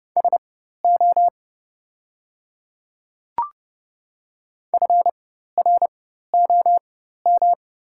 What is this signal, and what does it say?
Morse code, heard as a keyed tone, about 30 wpm, spaced as for 14 wpm, 700 Hz. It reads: SO FROM
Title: 4x Character Spacing - Top 100 Words - 30wpm